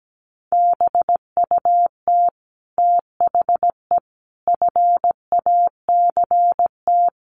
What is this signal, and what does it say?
Morse code, heard as a keyed tone, about 17 wpm, 700 Hz. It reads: BUT THE FACT
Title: Sets of 5 Words Encore2 17wpm